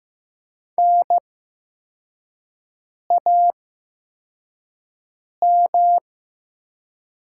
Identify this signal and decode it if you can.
Morse code, heard as a keyed tone, about 15 wpm, spaced as for 4 wpm, 700 Hz. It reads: NAM
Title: 8x Character Spacing - Top 500 Words - 15wpm